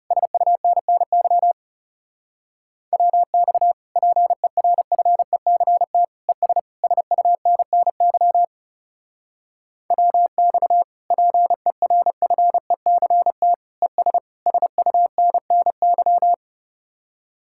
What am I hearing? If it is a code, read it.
Morse code, heard as a keyed tone, about 36 wpm, 700 Hz. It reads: SUNNY WX PERFECT ES SUNNY WX PERFECT ES SUNNY
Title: Speed-Racing Common QSO Phrases Encore - 30wpm